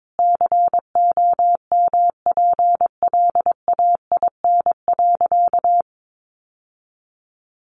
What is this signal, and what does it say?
Morse code, heard as a keyed tone, about 22 wpm, 700 Hz. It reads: COMPLAIN.
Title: Sentences from Top 1300 Words 22wpm